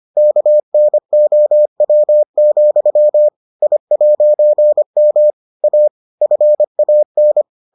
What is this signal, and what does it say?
Morse code, heard as a keyed tone, about 25 wpm, 600 Hz. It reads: KNOW, I'M A FAN